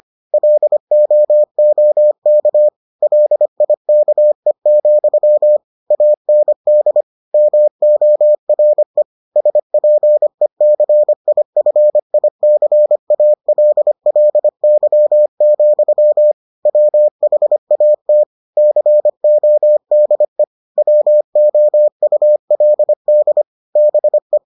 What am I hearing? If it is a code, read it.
Morse code, heard as a keyed tone, about 25 wpm, 600 Hz. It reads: LOOK LIKE, AND MORE SPECIFICALLY, WHAT CODE WOULD BE